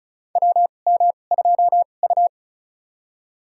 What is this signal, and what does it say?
Morse code, heard as a keyed tone, about 34 wpm, spaced as for 25 wpm, 700 Hz. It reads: WM2U
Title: USA Call Signs - 4 Characters - 2x Character Spacing - 35wpm